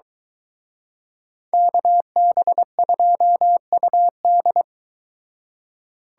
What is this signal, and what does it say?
Morse code, heard as a keyed tone, about 23 wpm, 700 Hz. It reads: KB2UD